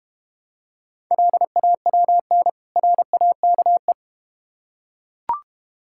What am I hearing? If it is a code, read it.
Morse code, heard as a keyed tone, about 31 wpm, 700 Hz. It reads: LAWN RAKE
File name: Easy Words - Speed-Racing with Hints - 25wpm